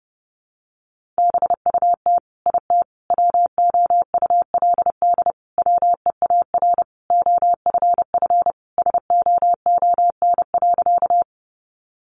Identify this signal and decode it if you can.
Morse code, heard as a keyed tone, about 30 wpm, 700 Hz. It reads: BUT IT WOULD WEAR OFF SOON.
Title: Sentences from Top 1500 Words 30wpm